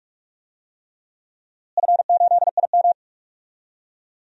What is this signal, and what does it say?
Morse code, heard as a keyed tone, about 41 wpm, 700 Hz. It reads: F8IM
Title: DXCC-100 Call Signs - 4 Characters - Repeated 3x - 45wpm